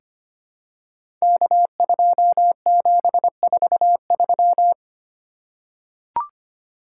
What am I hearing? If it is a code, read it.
Morse code, heard as a keyed tone, about 25 wpm, 700 Hz. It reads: K2743
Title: Code Groups - 5 Letter:Numbers - 25wpm